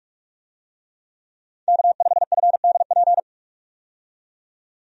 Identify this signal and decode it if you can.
Morse code, heard as a keyed tone, about 46 wpm, 700 Hz. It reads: K5FDP
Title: USA Call Signs - 5 Characters - 45wpm